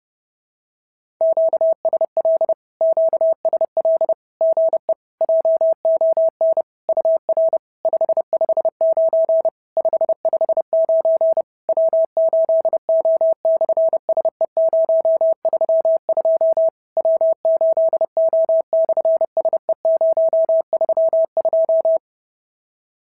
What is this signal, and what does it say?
Morse code, heard as a keyed tone, about 30 wpm, 650 Hz. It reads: QSL QSL GE JON UR 559 559 W8O/SE032 W8O/SE032
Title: SOTA QSO Practice 30wpm